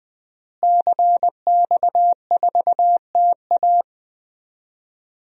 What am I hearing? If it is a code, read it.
Morse code, heard as a keyed tone, about 20 wpm, 700 Hz. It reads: CX4TA
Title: DXCC-100 Call Signs - 5 Characters - Repeated 3x - 20wpm